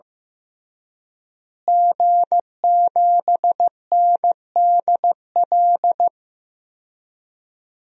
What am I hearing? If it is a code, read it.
Morse code, heard as a keyed tone, about 15 wpm, 700 Hz. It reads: G7NDL
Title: DXCC-100 Call Signs - 5 Characters - Repeated 3x - 15wpm